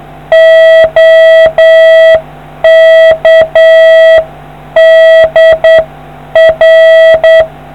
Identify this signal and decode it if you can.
Morse code, heard as a keyed tone, about 8 wpm, 650 Hz. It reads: OK DR